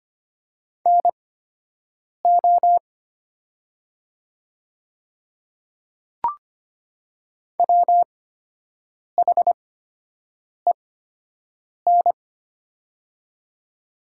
Morse code, heard as a keyed tone, about 25 wpm, spaced as for 7 wpm, 700 Hz. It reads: NO WHEN
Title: 8x Character Spacing - Top 100 Words - 25wpm